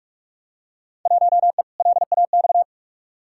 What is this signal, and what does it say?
Morse code, heard as a keyed tone, about 42 wpm, 700 Hz. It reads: 1E LAX
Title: Field Day Exchanges - 45wpm